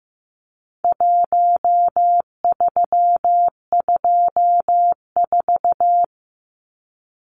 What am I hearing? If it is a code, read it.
Morse code, heard as a keyed tone, about 15 wpm, 700 Hz. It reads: 1324
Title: Numbers - Contest Style - 4-Digit 15wpm